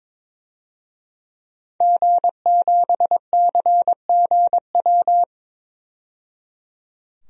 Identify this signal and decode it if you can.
Morse code, heard as a keyed tone, about 21 wpm, 700 Hz. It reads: G7CGW
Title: DXCC-100 Call Signs - 5 Characters - 22wpm